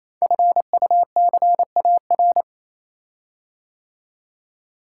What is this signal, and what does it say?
Morse code, heard as a keyed tone, about 28 wpm, 700 Hz. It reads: FUCAR
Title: Code Groups - 5 Letter:Numbers - 28wpm